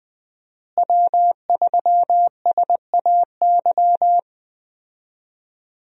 Morse code, heard as a keyed tone, about 20 wpm, 700 Hz. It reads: W3SAY
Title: USA Call Signs - 5 Characters - Repeated 2x - 20wpm